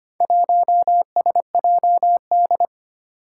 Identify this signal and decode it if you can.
Morse code, heard as a keyed tone, about 25 wpm, 700 Hz. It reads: 1SJD
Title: Four Challenging Letter-Number Combinations 25wpm